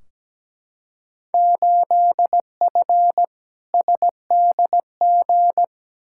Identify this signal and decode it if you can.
Morse code, heard as a keyed tone, about 17 wpm, 700 Hz. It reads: 8F SDG
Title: Field Day Exchanges Encore - 17wpm